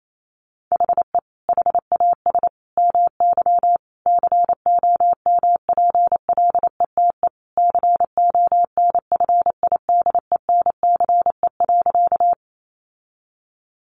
Morse code, heard as a keyed tone, about 28 wpm, 700 Hz. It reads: HE HAS MY COMPLETE CONFIDENCE.
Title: Sentences from Top 1500 Words 28wpm